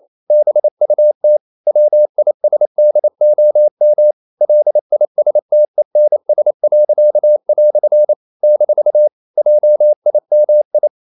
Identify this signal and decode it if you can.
Morse code, heard as a keyed tone, about 28 wpm, 600 Hz. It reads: BUT WISDOM LISTENS." - JIMI
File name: CW / Practice / EnglishQuotations0228